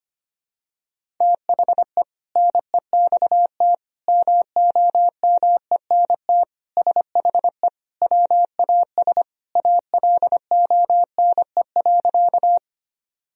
Morse code, heard as a keyed tone, about 25 wpm, 700 Hz. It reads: THE NEXT MOMENT SHE WAS ALONE.
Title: Sentences from Top 1000 Words - Review Part 1 - 25wpm